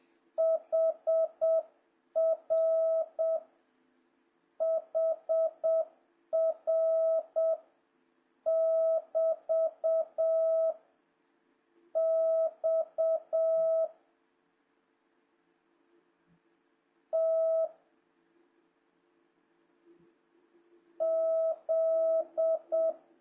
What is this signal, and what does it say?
Morse code, heard as a keyed tone, about 7 wpm, 660 Hz. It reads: HR HR <BT> X T Z
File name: jh8cbh / 2026cw